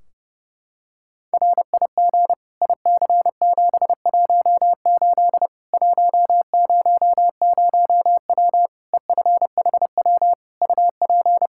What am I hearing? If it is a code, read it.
Morse code, heard as a keyed tone, about 30 wpm, 700 Hz. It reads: RIG IC718 100W EFHW UP
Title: Practice QSO - Exchange 2 - Info - 30wpm